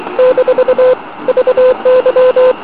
Morse code, heard as a keyed tone, about 23 wpm, 510 Hz, with noise. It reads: - VY